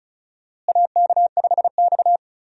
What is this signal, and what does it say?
Morse code, heard as a keyed tone, about 36 wpm, 700 Hz. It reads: AK5X
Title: USA Call Signs - 4 Characters - Repeated 2x - 35wpm